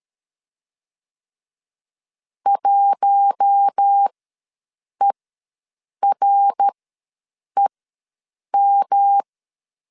Morse code, heard as a keyed tone, about 13 wpm, spaced as for 7 wpm, 800 Hz. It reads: 1EREM
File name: LEZIONI / 7 GF9Z / 7 G ws 12wpm 3140